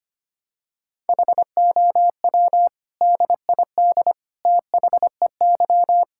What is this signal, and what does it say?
Morse code, heard as a keyed tone, about 25 wpm, 700 Hz. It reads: HOW DID THEY
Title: Sentences from Top 1000 Words - Review Part 5 - 25wpm